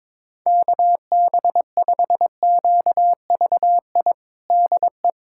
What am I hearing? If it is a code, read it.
Morse code, heard as a keyed tone, about 22 wpm, 700 Hz. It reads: KB5QVI DE
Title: USA Callsign DE Callsign - 6 Characters - 22wpm